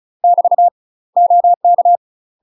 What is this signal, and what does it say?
Morse code, heard as a keyed tone, about 34 wpm, 700 Hz. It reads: <BT> OK
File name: UK MASTER QSOs TRAINING / 35WPM / UK MASTER QSOs 0043